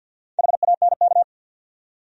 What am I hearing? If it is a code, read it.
Morse code, heard as a keyed tone, about 49 wpm, 700 Hz. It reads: HANK
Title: USA Common Names - 50wpm